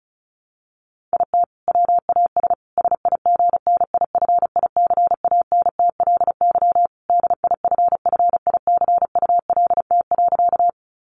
Morse code, heard as a keyed tone, about 34 wpm, 700 Hz. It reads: IT WAS SIGNIFICANTLY DIFFICULT.
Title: Sentences from Top 1500 Words 35wpm